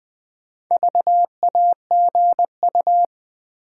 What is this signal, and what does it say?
Morse code, heard as a keyed tone, about 20 wpm, 700 Hz. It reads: VAGU
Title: Build A Word - 1 Letter At A Time - 5-Letter Words - 20wpm